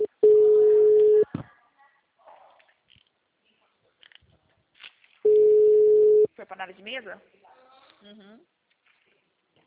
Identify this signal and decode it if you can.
Ringback tone (Europe, 425 Hz), with speech.